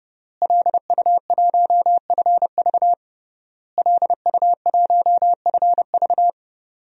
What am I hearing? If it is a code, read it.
Morse code, heard as a keyed tone, about 32 wpm, 700 Hz. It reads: LU1FV LU1FV
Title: DXCC-100 Call Signs - 5 Characters - Repeated 3x - 30wpm